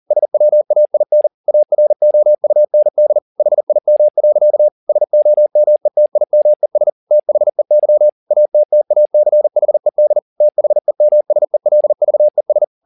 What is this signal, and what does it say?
Morse code, heard as a keyed tone, about 40 wpm, 600 Hz. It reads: SWAIN AROUND HIM. SOMETIMES THEY ATTACHED THEMSELVES
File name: the old man and the sea 2 40wpm